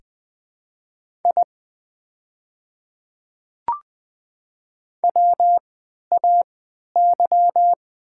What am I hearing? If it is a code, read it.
Morse code, heard as a keyed tone, about 20 wpm, spaced as for 11 wpm, 700 Hz. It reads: I WAY
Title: 3x Character Spacing - Top 100 Words - 20wpm